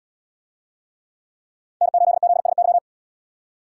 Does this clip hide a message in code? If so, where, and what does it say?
Yes: Morse code from 1.8 to 2.8 s, heard as a keyed tone, about 77 wpm, 700 Hz. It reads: N8ZSY